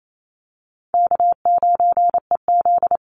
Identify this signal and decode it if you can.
Morse code, heard as a keyed tone, about 28 wpm, 700 Hz. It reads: K9EZ
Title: NA SOTA Chaser Callsigns 28wpm